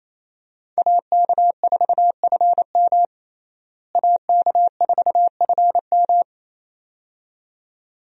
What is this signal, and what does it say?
Morse code, heard as a keyed tone, about 28 wpm, 700 Hz. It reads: AK4FM AK4FM
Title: USA Call Signs - 5 Characters - Repeated 2x - 28wpm